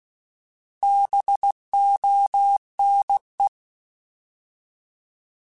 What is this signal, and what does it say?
Morse code, heard as a keyed tone, about 16 wpm, 800 Hz. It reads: BONE
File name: Alphabetical / Alphabetical mp3s / Bs @15